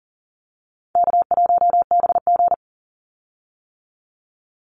Morse code, heard as a keyed tone, about 40 wpm, 700 Hz. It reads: K1BG
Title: Call Signs - 40wpm